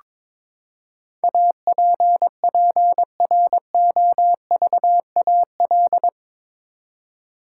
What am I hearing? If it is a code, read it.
Morse code, heard as a keyed tone, about 22 wpm, 700 Hz. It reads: APPROVAL
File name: practice / 22wpm / Eight Letter Words - 22wpm